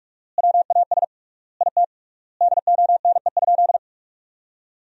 Morse code, heard as a keyed tone, about 48 wpm, 700 Hz. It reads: WAS IT DONE?
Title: 3x Word-Spacing - Sentences from Top 500 Words - 45wpm